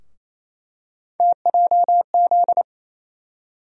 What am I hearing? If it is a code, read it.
Morse code, heard as a keyed tone, about 28 wpm, 700 Hz. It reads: TJZ
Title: USA Call Sign Postfixes - 3 Letters 28wpm